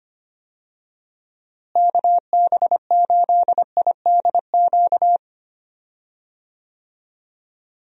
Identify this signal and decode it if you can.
Morse code, heard as a keyed tone, about 25 wpm, 700 Hz. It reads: KB8IDQ
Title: USA Call Signs - 6 Characters - Repeated 3x - 25wpm